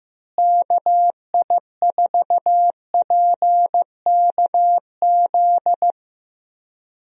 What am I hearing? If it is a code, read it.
Morse code, heard as a keyed tone, about 15 wpm, 700 Hz. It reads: KI4PKZ
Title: USA Call Signs - 6 Characters - Repeated 3x - 15wpm